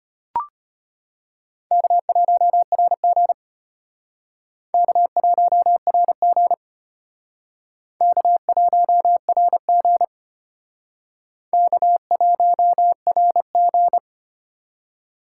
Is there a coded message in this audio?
Yes: Morse code from 1.7 to 14.0 s, heard as a keyed tone, about 34 wpm, 700 Hz. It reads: K1RG K1RG K1RG K1RG